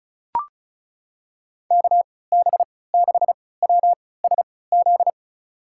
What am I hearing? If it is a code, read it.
Morse code, heard as a keyed tone, about 36 wpm, spaced as for 20 wpm, 700 Hz. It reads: KB6WSZ
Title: USA Call Signs - 6 Characters - 3x Character Spacing - 35wpm